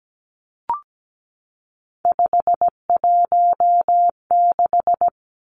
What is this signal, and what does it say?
Morse code, heard as a keyed tone, about 17 wpm, 700 Hz. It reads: 516
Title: Three Challenging Letter-Number Combinations 17wpm